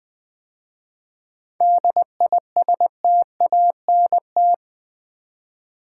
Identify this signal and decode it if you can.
Morse code, heard as a keyed tone, about 20 wpm, 700 Hz. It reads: DISTANT
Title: 7 Letter Words - First 3 Letters Given - 20wpm